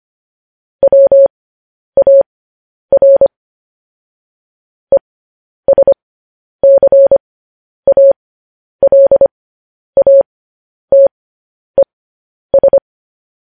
Morse code, heard as a keyed tone, about 25 wpm, spaced as for 10 wpm, 550 Hz. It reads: WAR ESCALATES
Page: news headlines in morse code at 10 wpm